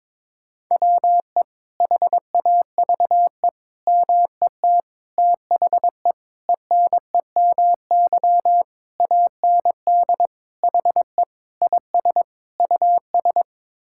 Morse code, heard as a keyed tone, about 22 wpm, 700 Hz. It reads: WE HAVE MET THE ENEMY AND HE IS US
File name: Sets of 9 Words - 22wpm